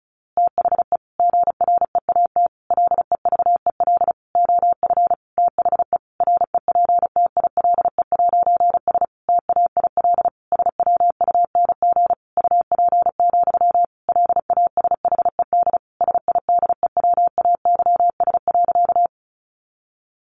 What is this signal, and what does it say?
Morse code, heard as a keyed tone, about 34 wpm, 700 Hz. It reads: THE GREAT LEVEL OF THE REPTILE'S TAIL SWUNG UP, LASHED SIDEWAYS.
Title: A Sound Of Thunder 35wpm